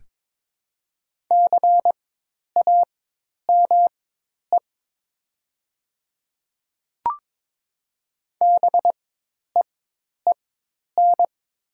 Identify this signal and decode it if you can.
Morse code, heard as a keyed tone, about 22 wpm, spaced as for 10 wpm, 700 Hz. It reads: CAME BEEN
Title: 4x Character Spacing - Top 200 Words - 22wpm